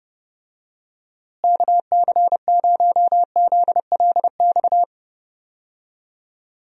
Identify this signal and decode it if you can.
Morse code, heard as a keyed tone, about 30 wpm, 700 Hz. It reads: KC0ZLX
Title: USA Call Signs - 6 Characters - Repeated 2x - 30wpm